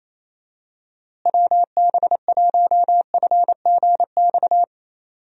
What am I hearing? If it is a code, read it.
Morse code, heard as a keyed tone, about 27 wpm, 700 Hz. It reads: WB1FGX
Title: USA Call Signs - 6 Characters - Repeated 3x - 28wpm